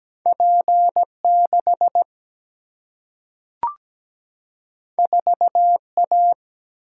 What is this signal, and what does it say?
Morse code, heard as a keyed tone, about 17 wpm, 700 Hz. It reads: P6 4A
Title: Single-Letter Single-Number 17wpm